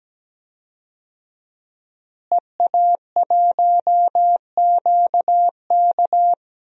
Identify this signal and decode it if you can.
Morse code, heard as a keyed tone, about 17 wpm, 700 Hz. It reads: EA1QK
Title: DXCC-100 Call Signs - 5 Characters - Repeated 3x - 17wpm